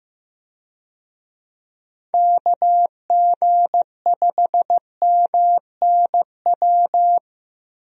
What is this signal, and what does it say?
Morse code, heard as a keyed tone, about 15 wpm, 700 Hz. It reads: KG5MNW
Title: USA Call Signs - 6 Characters - 15wpm